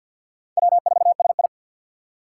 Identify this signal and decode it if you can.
Morse code, heard as a keyed tone, about 53 wpm, 700 Hz. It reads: W4SI
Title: USA Call Signs - 4 Characters - 2x Character Spacing - 50wpm